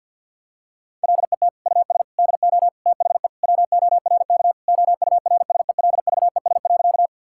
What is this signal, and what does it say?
Morse code, heard as a keyed tone, about 51 wpm, 700 Hz. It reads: LET US DO THE WORK OURSELVES.
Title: Sentences from Top 1600 Words 50wpm